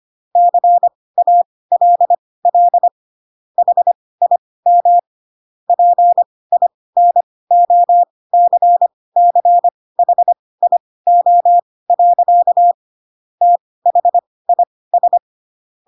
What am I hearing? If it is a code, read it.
Morse code, heard as a keyed tone, about 25 wpm, spaced as for 18 wpm, 700 Hz. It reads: CALL HIM PINOCCHIO. THIS
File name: Pinocchio ch3 12 25 ik1hge